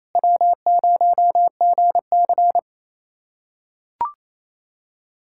Morse code, heard as a keyed tone, about 28 wpm, 700 Hz. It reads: W0GC
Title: USA Callsign DE Callsign - 4 Characters - 28wpm